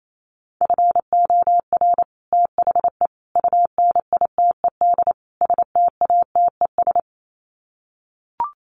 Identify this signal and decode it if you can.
Morse code, heard as a keyed tone, about 28 wpm, 700 Hz. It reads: FOR THE UNITED STATES
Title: Sets of 4 Words 28wpm